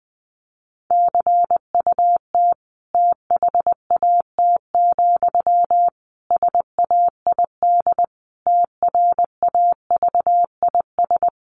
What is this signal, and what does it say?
Morse code, heard as a keyed tone, about 20 wpm, 700 Hz. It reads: CUT THAT, SAID TRAVIS
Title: A Sound Of Thunder 20wpm